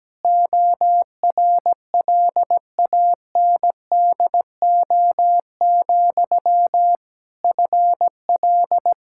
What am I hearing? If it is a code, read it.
Morse code, heard as a keyed tone, about 17 wpm, 700 Hz. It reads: ORLANDO, FL
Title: Practice QSO - Exchange 1 - Beginning - 17wpm